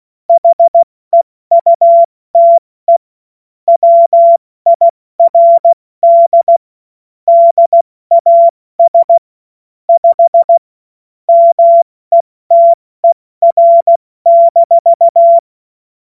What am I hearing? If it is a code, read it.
Morse code, heard as a keyed tone, about 16 wpm, spaced as for 14 wpm, 680 Hz. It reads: HEUTE WIRD DAS 5 METER-
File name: Morsekurs / Witze / Witz 21. September 2021, 70 F80